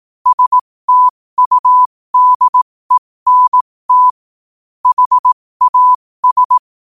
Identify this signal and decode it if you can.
Morse code, heard as a keyed tone, about 18 wpm, spaced as for 15 wpm, 1000 Hz. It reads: STUDENT HAS
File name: qotdfiles / 15wpm